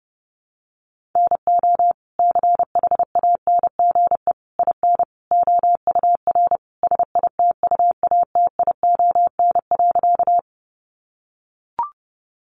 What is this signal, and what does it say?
Morse code, heard as a keyed tone, about 30 wpm, 700 Hz. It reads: NO CHANGE IN OUR SITUATION.
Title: Sentences from Top 1200 Words 30wpm